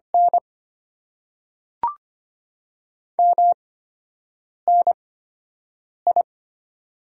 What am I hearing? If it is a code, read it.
Morse code, heard as a keyed tone, about 25 wpm, spaced as for 7 wpm, 700 Hz. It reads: N MNI
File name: Top 90 QSO Elements - 8x Character Spacing - 25wpm